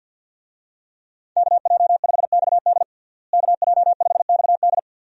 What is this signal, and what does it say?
Morse code, heard as a keyed tone, about 52 wpm, 700 Hz. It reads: KJ5XD KJ5XD
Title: USA Call Signs - 5 Characters - Repeated 2x - 50wpm